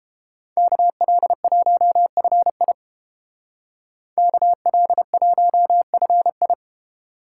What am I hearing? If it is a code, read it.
Morse code, heard as a keyed tone, about 27 wpm, 700 Hz. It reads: KL1FI KL1FI
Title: Speed Racing - USA 2x2 Call Signs 22wpm